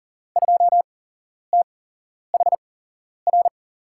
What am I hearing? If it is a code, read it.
Morse code, heard as a keyed tone, about 39 wpm, spaced as for 11 wpm, 700 Hz. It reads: 2THR
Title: USA Call Signs - 8x Character Spacing - 40wpm